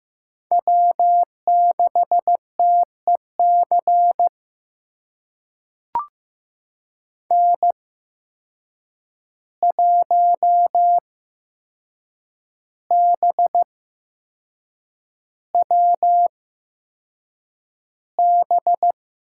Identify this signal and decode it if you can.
Morse code, heard as a keyed tone, about 15 wpm, spaced as for 6 wpm, 700 Hz. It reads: W6TEC N1BWB
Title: USA Call Signs - 5 Characters - 8x Character Spacing - 15wpm